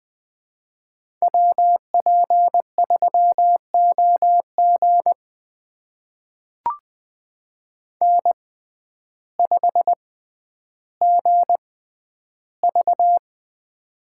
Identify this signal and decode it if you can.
Morse code, heard as a keyed tone, about 20 wpm, spaced as for 11 wpm, 700 Hz. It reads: WP3OG N5GV
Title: USA Call Signs - 6x Character Spacing - 20wpm